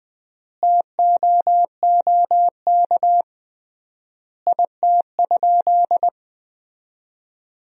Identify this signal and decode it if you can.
Morse code, heard as a keyed tone, about 20 wpm, 700 Hz. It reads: TOOK IT?
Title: 3x Word-Spacing - Sentences from Top 300 Words - 20wpm